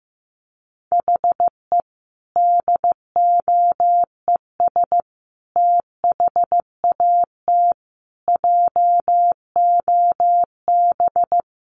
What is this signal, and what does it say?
Morse code, heard as a keyed tone, about 15 wpm, 700 Hz. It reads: HE DOES THAT JOB
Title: Sentences from Top 700 Words 15wpm